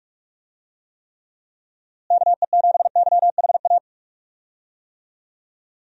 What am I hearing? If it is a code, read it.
Morse code, heard as a keyed tone, about 46 wpm, 700 Hz. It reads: KE7YHA
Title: USA Call Signs - 6 Characters - Repeated 3x - 45wpm